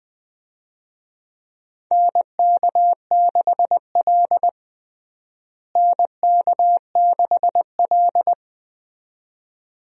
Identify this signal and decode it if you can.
Morse code, heard as a keyed tone, about 20 wpm, 700 Hz. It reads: NK6L NK6L
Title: USA Call Signs - 4 Characters - Repeated 2x - 20wpm